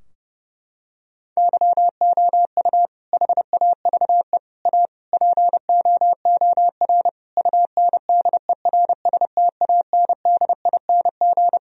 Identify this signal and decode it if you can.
Morse code, heard as a keyed tone, about 30 wpm, 700 Hz. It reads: YOU HAVE A POOR UNDERSTANDING
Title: Sentences from Top 1000 Words - Review Part 3 - 30wpm